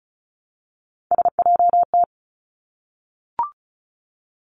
Morse code, heard as a keyed tone, about 36 wpm, 700 Hz. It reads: SJT
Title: Three Challenging Letter-Number Combinations 35wpm